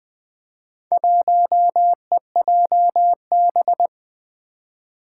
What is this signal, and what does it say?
Morse code, heard as a keyed tone, about 21 wpm, 700 Hz. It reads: 1EJB
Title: Four Challenging Letter-Number Combinations 20wpm